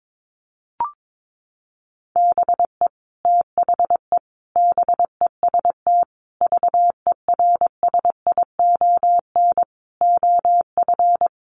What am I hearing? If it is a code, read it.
Morse code, heard as a keyed tone, about 22 wpm, 700 Hz. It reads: BE THE BEST VERSION OF
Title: Sentences from Top 1400 Words 22wpm